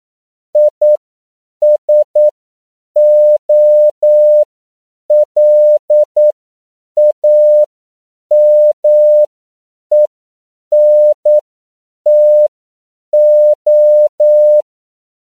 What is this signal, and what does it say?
Morse code, heard as a keyed tone, about 9 wpm, spaced as for 7 wpm, 600 Hz. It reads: ISOLAMENTO
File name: CW / 10PPM